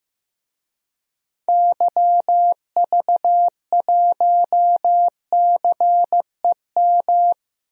Morse code, heard as a keyed tone, about 15 wpm, 700 Hz. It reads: YV1CEM
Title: DXCC-100 Call Signs - 6 Characters - Repeated 3x - 15wpm